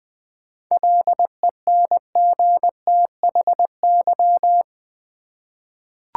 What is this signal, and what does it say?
Morse code, heard as a keyed tone, about 20 wpm, 690 Hz. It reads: LENGTHY
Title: 7 Letter Words - First 3 Letters Given - 20wpm